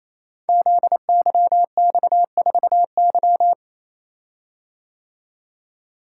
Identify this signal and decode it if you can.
Morse code, heard as a keyed tone, about 28 wpm, 700 Hz. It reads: ZYX4Y
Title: Code Groups - 5 Letter:Numbers - 28wpm